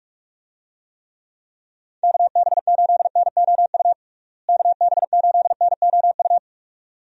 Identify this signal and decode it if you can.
Morse code, heard as a keyed tone, about 46 wpm, 700 Hz. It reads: KB8NOU KB8NOU
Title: USA Call Signs - 6 Characters - Repeated 2x - 45wpm